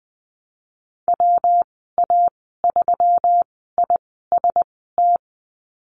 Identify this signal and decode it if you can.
Morse code, heard as a keyed tone, about 19 wpm, spaced as for 14 wpm, 700 Hz. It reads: WA3IST
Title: 128 Lesson 21g - Number 6 - Call Signs 20wpm